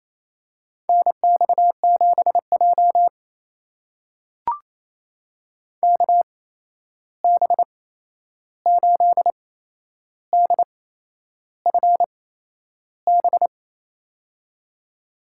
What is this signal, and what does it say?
Morse code, heard as a keyed tone, about 28 wpm, spaced as for 11 wpm, 700 Hz. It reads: NX7J KB8DFB
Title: USA Call Signs - 8x Character Spacing - 28wpm